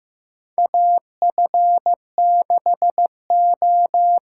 Morse code, heard as a keyed tone, about 15 wpm, 700 Hz. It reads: AF6O
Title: USA Call Signs - 6x Character Spacing - 15wpm